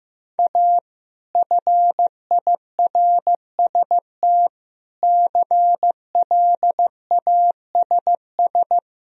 Morse code, heard as a keyed tone, about 16 wpm, 700 Hz. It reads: A FIRST CLASS
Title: Sentences from Top 1000 Words - Review Part 2 - 15wpm